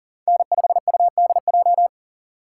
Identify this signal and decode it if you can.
Morse code, heard as a keyed tone, about 40 wpm, 700 Hz. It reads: N5UDJ
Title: USA Call Signs - 6x Character Spacing - 40wpm